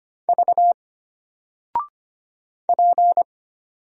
Morse code, heard as a keyed tone, about 25 wpm, 700 Hz. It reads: V P
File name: Single-Letter-Number-25wpm